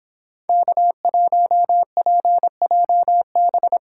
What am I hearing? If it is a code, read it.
Morse code, heard as a keyed tone, about 29 wpm, 700 Hz. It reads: K1PJB